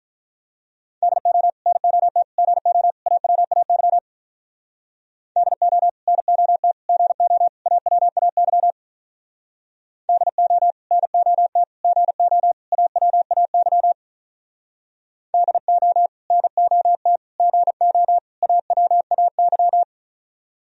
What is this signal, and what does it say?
Morse code, heard as a keyed tone, about 43 wpm, 700 Hz. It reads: DO NOT GO AWAY DO NOT GO AWAY DO NOT GO AWAY DO NOT GO AWAY
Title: Speed-Racing - Sets of 4 Words 35wpm